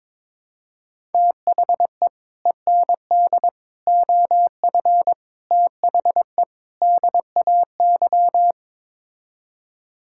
Morse code, heard as a keyed tone, about 22 wpm, 700 Hz. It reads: THE END OF THE DAY
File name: practice / 22wpm / Build A Phrase - 5-Word Phrases - 22wpm